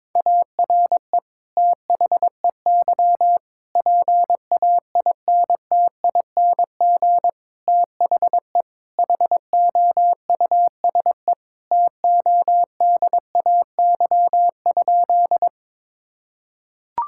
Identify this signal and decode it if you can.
Morse code, heard as a keyed tone, about 22 wpm, 700 Hz. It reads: ARE THEY PAINTING THE HOUSE TODAY?
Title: Sentences from Top 1000 Words - Review Part 6 - 22wpm